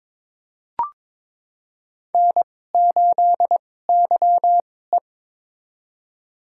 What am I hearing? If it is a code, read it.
Morse code, heard as a keyed tone, about 22 wpm, spaced as for 16 wpm, 700 Hz. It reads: N8YE
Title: USA Call Signs - 2x Character Spacing - 22wpm